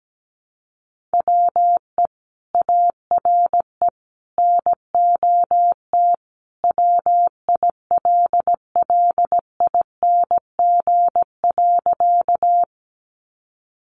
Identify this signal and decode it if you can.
Morse code, heard as a keyed tone, about 17 wpm, 700 Hz. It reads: WE ARE NOT WILLING.
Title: Sentences from Top 1000 Words 17wpm